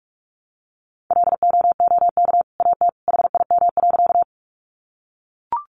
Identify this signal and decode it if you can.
Morse code, heard as a keyed tone, about 44 wpm, 700 Hz. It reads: LOOK AT HIM.
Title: Sentences from Top 100 Words 45wpm